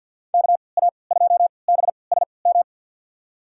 Morse code, heard as a keyed tone, about 53 wpm, spaced as for 29 wpm, 700 Hz. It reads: KA2BSM
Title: USA Call Signs - 6 Characters - 3x Character Spacing - 50wpm